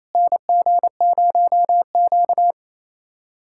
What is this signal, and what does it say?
Morse code, heard as a keyed tone, about 28 wpm, 700 Hz. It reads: NG0Q
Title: USA Callsign DE Callsign - 4 Characters - 28wpm